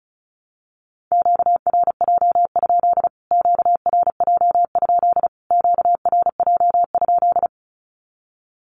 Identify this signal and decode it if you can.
Morse code, heard as a keyed tone, about 34 wpm, 700 Hz. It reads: QRJ? QRJ? QRJ?
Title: Q-Codes Repeated 3x 35wpm